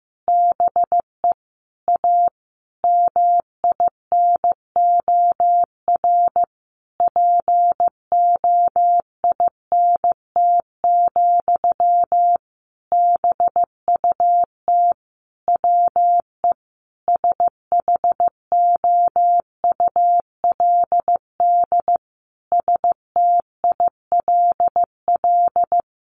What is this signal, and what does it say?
Morse code, heard as a keyed tone, about 14 wpm, 700 Hz. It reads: BE A MINOR POINT, BUT WE SHOULD STILL
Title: Sentences from Top 1500 Words 15wpm